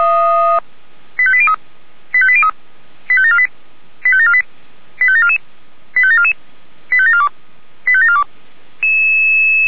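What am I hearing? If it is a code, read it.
Five-tone selective call (ZVEI), heard as quick runs of five tones, tones of about 70 ms. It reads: 85803 85803 85748 85748 85744 85744 85742 85742
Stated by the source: ZVEI five-tone call received on SDR.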